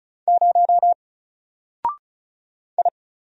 Morse code, heard as a keyed tone, about 34 wpm, 700 Hz. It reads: 0 I
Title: Single Character (Letter:Number:Punctuation:Prosign) Alt 35wpm